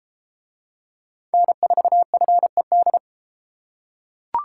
Morse code, heard as a keyed tone, about 32 wpm, 700 Hz. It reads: N4FED